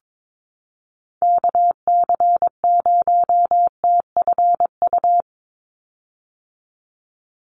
Morse code, heard as a keyed tone, about 22 wpm, 700 Hz. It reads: KC0TFU
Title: USA Call Signs 22wpm